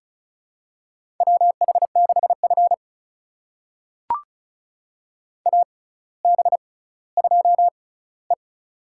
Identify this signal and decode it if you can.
Morse code, heard as a keyed tone, about 34 wpm, spaced as for 18 wpm, 700 Hz. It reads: WH6F AB2E
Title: USA Call Signs - 4 Characters - 6x Character Spacing - 35wpm